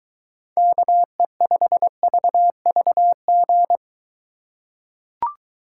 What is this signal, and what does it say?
Morse code, heard as a keyed tone, about 23 wpm, 700 Hz. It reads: KE5VVG